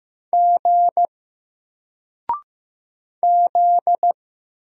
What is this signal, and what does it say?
Morse code, heard as a keyed tone, about 15 wpm, 700 Hz. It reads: G Z
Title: Single Character (Letter:Number:Punctuation:Prosign) 15wpm